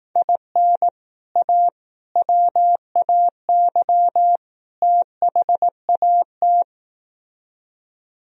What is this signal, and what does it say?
Morse code, heard as a keyed tone, about 18 wpm, 700 Hz. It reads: IN A WAY THAT
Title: Speed-Racing - Sets of 4 Words 15wpm